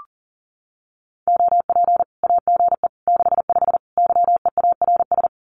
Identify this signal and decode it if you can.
Morse code, heard as a keyed tone, about 40 wpm, 700 Hz. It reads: OP AGE 65 YEARS
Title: Common QSO Phrases 40wpm